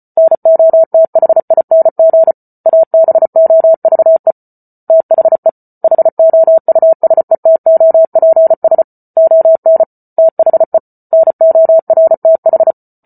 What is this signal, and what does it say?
Morse code, heard as a keyed tone, about 34 wpm, 650 Hz. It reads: NOTHING ABOVE THE HOUSETOPS ON THE NORTH
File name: Book Two / 08-Chapter 8 - 34 WPM